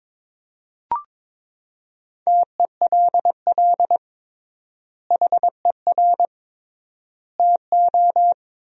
Morse code, heard as a keyed tone, about 22 wpm, 700 Hz. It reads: TELL HER TO
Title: 3x Word-Spacing - Sentences from Top 300 Words - 22wpm